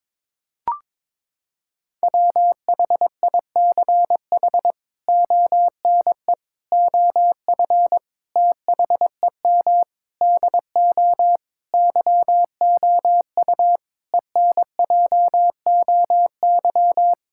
Morse code, heard as a keyed tone, about 22 wpm, 700 Hz. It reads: WHICH ONE OF THEM DO YOU ENJOY
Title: Sentences from Top 1000 Words - Review Part 5 - 22wpm